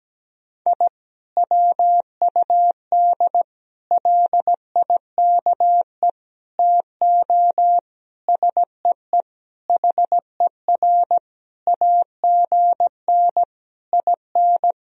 Morse code, heard as a keyed tone, about 17 wpm, 700 Hz. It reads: I WUD LIKE TO SEE HER AGN IN
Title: Sentences from Top 200 Words and Abbreviations 17wpm